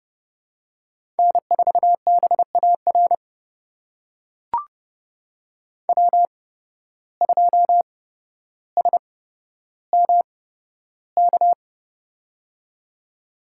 Morse code, heard as a keyed tone, about 30 wpm, spaced as for 13 wpm, 700 Hz. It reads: N4BAR W2SMK
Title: USA Call Signs - 5 Characters - 8x Character Spacing - 30wpm